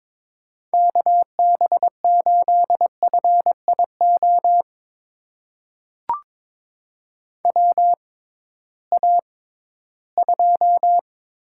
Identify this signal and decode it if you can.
Morse code, heard as a keyed tone, about 22 wpm, spaced as for 14 wpm, 700 Hz. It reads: KB8FIO WA2
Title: USA Call Signs - 6 Characters - 6x Character Spacing - 22wpm